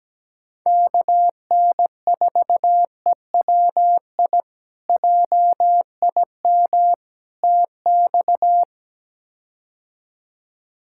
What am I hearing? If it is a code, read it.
Morse code, heard as a keyed tone, about 17 wpm, 700 Hz. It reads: KN4EWI JIM TX
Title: K1USN SST QSO Practice - 17wpm